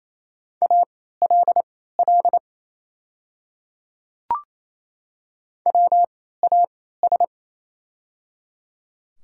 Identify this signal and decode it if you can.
Morse code, heard as a keyed tone, about 28 wpm, spaced as for 16 wpm, 700 Hz. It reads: ALL WAS
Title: 3x Character Spacing - Top 200 Words - 28wpm